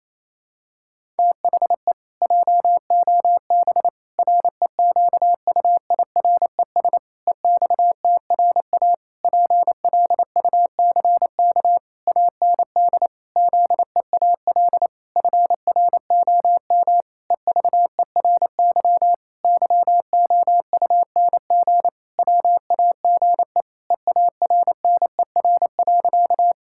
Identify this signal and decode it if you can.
Morse code, heard as a keyed tone, about 28 wpm, 700 Hz. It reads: THE JOB REQUIRES EXTRA PLUCK AND ZEAL FROM EVERY YOUNG WAGE EARNER.
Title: Pangram Challenge v2 - 28wpm